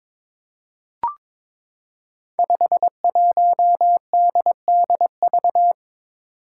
Morse code, heard as a keyed tone, about 22 wpm, 700 Hz. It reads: 51DDV
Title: Five Challenging Letter-Number Combinations 22wpm